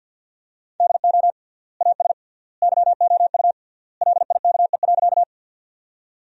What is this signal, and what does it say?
Morse code, heard as a keyed tone, about 50 wpm, 700 Hz. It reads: DO AS YOU LIKE.
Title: 3x Word-Spacing - Sentences from Top 100 Words - 50wpm